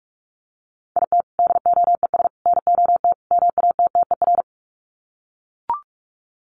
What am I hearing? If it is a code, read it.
Morse code, heard as a keyed tone, about 44 wpm, 700 Hz. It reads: IT DOES NOT MATTER
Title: Sets of 4 Words 45wpm